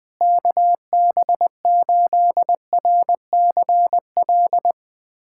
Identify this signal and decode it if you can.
Morse code, heard as a keyed tone, about 20 wpm, 700 Hz. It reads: KB8RCL
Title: USA Callsign DE Callsign 20wpm